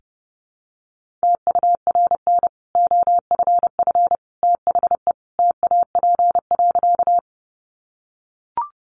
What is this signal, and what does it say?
Morse code, heard as a keyed tone, about 30 wpm, 700 Hz. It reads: TURN OFF THE TAP.
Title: Sentences from Top 1400 Words 30wpm